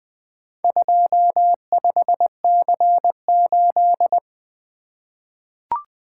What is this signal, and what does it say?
Morse code, heard as a keyed tone, about 20 wpm, 700 Hz. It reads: 25C8
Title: Four Letter-Number Combinations - 20wpm